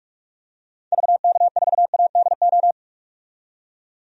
Morse code, heard as a keyed tone, about 46 wpm, 700 Hz. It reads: VK4ADO
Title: DXCC-100 Call Signs - 6 Characters - 45wpm